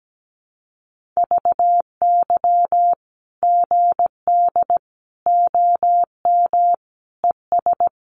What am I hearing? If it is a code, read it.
Morse code, heard as a keyed tone, about 17 wpm, 700 Hz. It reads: VY GD OM ES
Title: Common QSO Phrases 17wpm